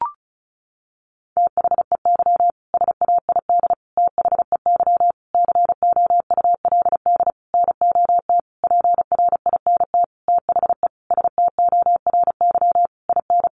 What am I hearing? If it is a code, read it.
Morse code, heard as a keyed tone, about 34 wpm, 700 Hz. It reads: THEY SAID THEY COULD NOT PRINT THE STORY IN
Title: Sentences from Top 1400 Words 35wpm